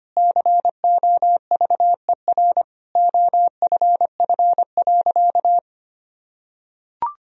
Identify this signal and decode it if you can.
Morse code, heard as a keyed tone, about 25 wpm, 700 Hz. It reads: COVER OFF.
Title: Sentences from Top 1000 Words - Review Part 1 - 25wpm